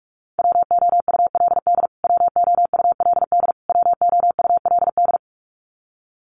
Morse code, heard as a keyed tone, about 44 wpm, 700 Hz. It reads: WOULD WOULD WOULD
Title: Top 200 Words Repeated 45wpm